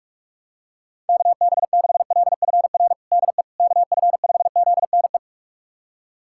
Morse code, heard as a keyed tone, about 46 wpm, 700 Hz. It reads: KB6LFR DE KF5ZNE
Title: USA Callsign DE Callsign 45wpm